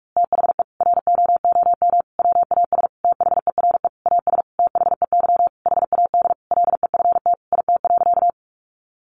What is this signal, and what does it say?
Morse code, heard as a keyed tone, about 44 wpm, 700 Hz. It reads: THE ROOM WAS THERE AS THEY HAD LEFT IT.
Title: A Sound Of Thunder 45wpm